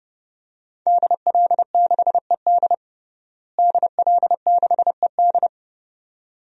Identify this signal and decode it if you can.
Morse code, heard as a keyed tone, about 29 wpm, 700 Hz. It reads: DL6ED DL6ED
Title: DXCC-100 Call Signs - 5 Characters - Repeated 2x - 30wpm